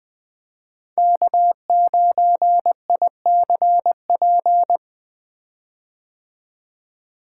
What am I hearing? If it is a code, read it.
Morse code, heard as a keyed tone, about 20 wpm, 700 Hz. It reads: K9ICP
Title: 2023 POTA Hunter Call Signs - 20wpm